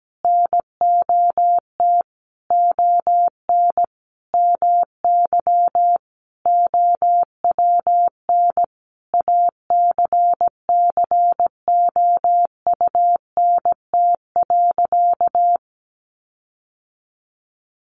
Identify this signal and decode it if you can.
Morse code, heard as a keyed tone, about 17 wpm, 700 Hz. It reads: NOT ON MY OWN ACCOUNT.
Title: Sentences from Top 1400 Words 17wpm